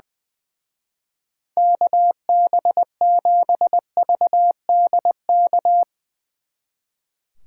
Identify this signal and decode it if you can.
Morse code, heard as a keyed tone, about 23 wpm, 700 Hz. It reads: KB7VDK
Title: Speed Racing - USA 2x3 Call Signs 20wpm